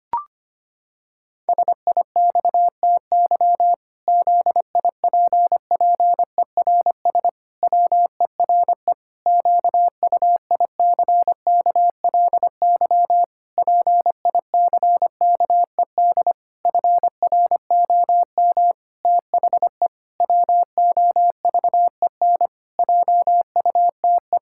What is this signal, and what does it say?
Morse code, heard as a keyed tone, about 25 wpm, 700 Hz. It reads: SIXTY ZIPPERS WERE QUICKLY PICKED FROM THE WOVEN JUTE
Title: Pangram Challenge v2 - 25wpm